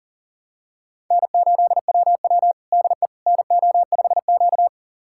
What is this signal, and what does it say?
Morse code, heard as a keyed tone, about 40 wpm, 700 Hz. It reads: N8WW DE NO5Q
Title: USA Callsign DE Callsign - 4 Characters - 40wpm